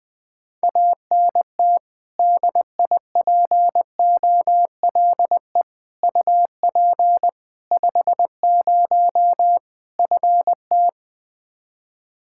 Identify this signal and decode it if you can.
Morse code, heard as a keyed tone, about 20 wpm, 700 Hz. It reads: ANT DIPOLE UP 50 FT
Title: Practice QSO - Exchange 2 - Info - 20wpm